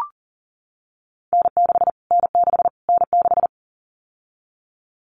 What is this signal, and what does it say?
Morse code, heard as a keyed tone, about 40 wpm, 700 Hz. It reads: N6 N6 N6
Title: Callsign Prefixes for Top 25 DXCC - Repeated 3x - 40wpm